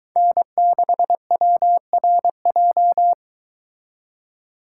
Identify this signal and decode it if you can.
Morse code, heard as a keyed tone, about 22 wpm, 700 Hz. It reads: N6WRJ